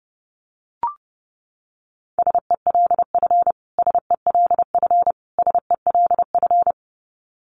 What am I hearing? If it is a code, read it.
Morse code, heard as a keyed tone, about 30 wpm, 700 Hz. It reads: SELF SELF SELF
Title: Top 500 Words Repeated 30wpm